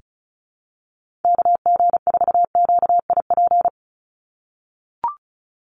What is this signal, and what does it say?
Morse code, heard as a keyed tone, about 34 wpm, 700 Hz. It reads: KG4QIP
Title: USA Call Signs 35wpm